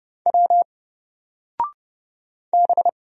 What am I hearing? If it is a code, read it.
Morse code, heard as a keyed tone, about 30 wpm, 700 Hz. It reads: W B
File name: Single-Letter-Number-30wpm